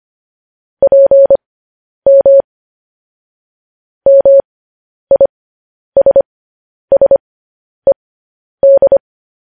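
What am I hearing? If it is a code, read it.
Morse code, heard as a keyed tone, about 25 wpm, spaced as for 10 wpm, 550 Hz. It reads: PM MISSED
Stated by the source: Morse code news headlines sent at 10 words per minute.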